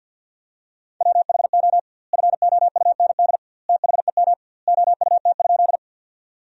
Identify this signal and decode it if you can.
Morse code, heard as a keyed tone, about 50 wpm, 700 Hz. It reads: WHO FOUND THEM OUT?
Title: 2x Word-Spacing - Sentences from Top 300 Words - 50wpm